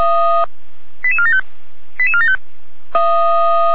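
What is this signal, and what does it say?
Five-tone selective call (ZVEI), heard as quick runs of five tones, tones of about 80 ms. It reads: 88475 88475
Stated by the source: ZVEI five-tone call received by SDR, from feuerwehr-orlamuende.org (fire brigade Orlamünde).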